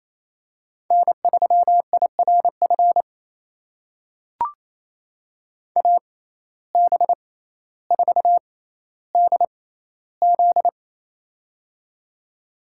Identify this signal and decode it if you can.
Morse code, heard as a keyed tone, about 28 wpm, spaced as for 14 wpm, 700 Hz. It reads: N3IRF AB4DZ
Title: USA Call Signs - 5 Characters - 6x Character Spacing - 28wpm